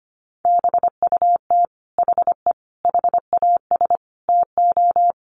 Morse code, heard as a keyed tone, about 25 wpm, 700 Hz. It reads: BUT HE HAS TO
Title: Sentences from Top 600 Words 25wpm